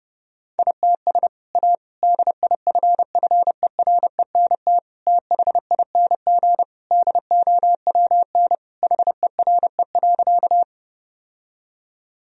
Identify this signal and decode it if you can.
Morse code, heard as a keyed tone, about 32 wpm, 700 Hz. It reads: ITS A DIFFERENT THING DOWN HERE.
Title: Sentences from Top 1000 Words - Review Part 3 - 30wpm